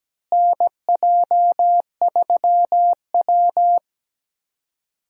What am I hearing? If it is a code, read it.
Morse code, heard as a keyed tone, about 17 wpm, 700 Hz. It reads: NJ3W
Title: USA Call Signs - 4 Characters - Repeated 2x - 17wpm